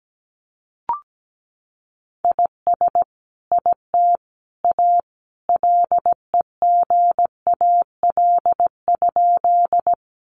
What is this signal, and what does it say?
Morse code, heard as a keyed tone, about 17 wpm, 700 Hz. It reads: IS IT A LEGAL?
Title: Sentences from Top 1200 Words 17wpm